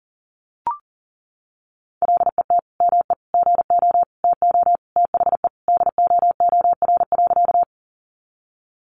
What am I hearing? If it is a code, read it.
Morse code, heard as a keyed tone, about 40 wpm, 700 Hz. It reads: LET ME GO TO THE DOOR.
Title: Sentences from Top 300 Words 40wpm